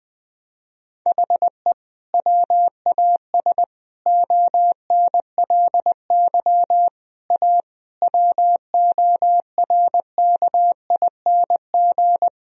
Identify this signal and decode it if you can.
Morse code, heard as a keyed tone, about 21 wpm, 700 Hz. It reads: HE WAS ONLY A WORKING
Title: Sentences from Top 1000 Words - Review Part 4 - 20wpm